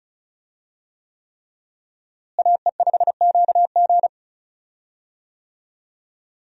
Morse code, heard as a keyed tone, about 36 wpm, 700 Hz. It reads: AE5QG
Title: USA Call Signs - 5 Characters - Repeated 3x - 35wpm